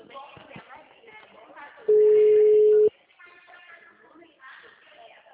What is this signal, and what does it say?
Ringback tone (Europe, 425 Hz), with speech.
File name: out- -2004- - - .7634